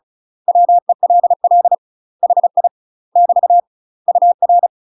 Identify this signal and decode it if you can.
Morse code, heard as a keyed tone, about 36 wpm, 700 Hz. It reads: WELL HI <BT> UR